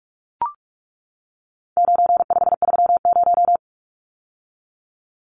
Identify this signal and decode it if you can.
Morse code, heard as a keyed tone, about 44 wpm, 700 Hz. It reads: 9530
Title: Numbers - 4-Digit 45wpm